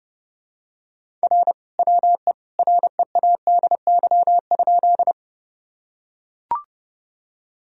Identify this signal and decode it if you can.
Morse code, heard as a keyed tone, about 30 wpm, 700 Hz. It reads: R WE READY?
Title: Sentences from Top 400 Words and Abbreviations 30wpm